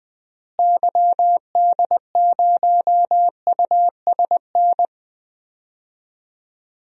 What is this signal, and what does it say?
Morse code, heard as a keyed tone, about 20 wpm, 700 Hz. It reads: YD0USN
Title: DXCC-100 Call Signs - 6 Characters - Repeated 3x - 20wpm